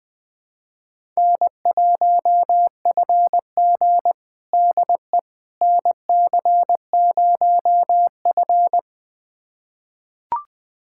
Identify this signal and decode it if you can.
Morse code, heard as a keyed tone, about 20 wpm, 700 Hz. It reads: N1FG DE NC0F
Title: USA Callsign DE Callsign - 4 Characters - 20wpm